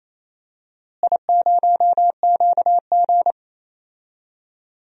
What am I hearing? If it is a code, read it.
Morse code, heard as a keyed tone, about 28 wpm, 700 Hz. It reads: I0QG
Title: DXCC-100 Call Signs - 4 Characters - Repeated 2x - 28wpm